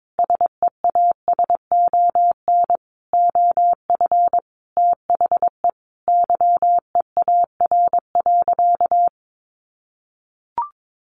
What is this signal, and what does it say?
Morse code, heard as a keyed tone, about 22 wpm, 700 Hz. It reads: SEASON OF THE YEAR.
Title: Sentences from Top 1400 Words 22wpm